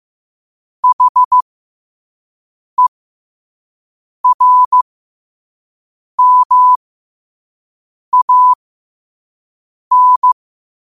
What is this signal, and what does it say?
Morse code, heard as a keyed tone, about 15 wpm, spaced as for 5 wpm, 1000 Hz. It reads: HERMAN